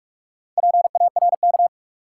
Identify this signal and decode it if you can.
Morse code, heard as a keyed tone, about 46 wpm, 700 Hz. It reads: PARK
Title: Sentences from Top 1300 Words - New Word Drill Alt - 45wpm